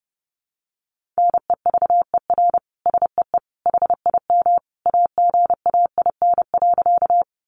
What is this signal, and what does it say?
Morse code, heard as a keyed tone, about 30 wpm, 700 Hz. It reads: NEVER SEE HIM AGAIN.
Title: Sentences from Top 300 Words 30wpm